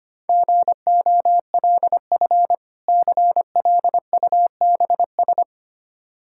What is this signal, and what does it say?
Morse code, heard as a keyed tone, about 25 wpm, 700 Hz. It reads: GOLF CLUBS
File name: practice / 25wpm / Difficult Words with Hints - 25wpm